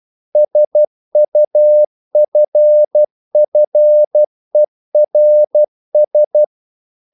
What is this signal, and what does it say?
Morse code, heard as a keyed tone, about 12 wpm, 600 Hz. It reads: SUFFERS
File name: CW / Practice / EnglishQuotations0112